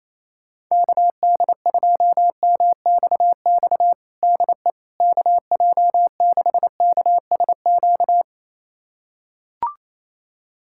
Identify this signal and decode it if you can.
Morse code, heard as a keyed tone, about 28 wpm, 700 Hz. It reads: KD2MXX DE KJ6KSQ
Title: USA Callsign DE Callsign 28wpm